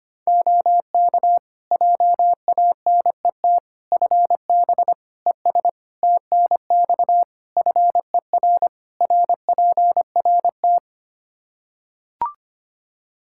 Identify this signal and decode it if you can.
Morse code, heard as a keyed tone, about 25 wpm, 700 Hz. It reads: OK JANET FB ES TNX FER RPRT
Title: Practice QSO - Exchange 2 - Info - 25wpm